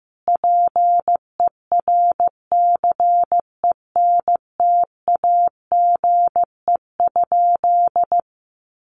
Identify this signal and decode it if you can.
Morse code, heard as a keyed tone, about 15 wpm, 700 Hz. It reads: PERCENTAGE?
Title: Sentences from Top 1300 Words 15wpm